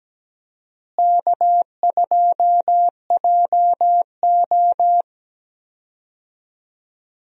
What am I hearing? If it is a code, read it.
Morse code, heard as a keyed tone, about 17 wpm, 700 Hz. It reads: K2JO
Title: USA Call Signs - 4 Characters - Repeated 2x - 17wpm